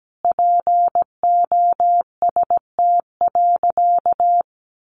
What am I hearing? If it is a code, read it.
Morse code, heard as a keyed tone, about 17 wpm, 700 Hz. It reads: POST.
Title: Sentences from Top 1000 Words 17wpm